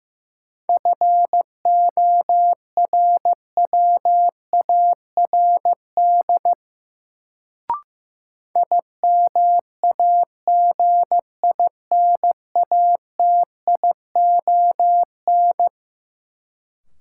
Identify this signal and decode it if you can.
Morse code, heard as a keyed tone, about 15 wpm, 700 Hz. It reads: FORWARD IMAGINATION
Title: Sentences from Top 1300 Words - New Word Drill - 15wpm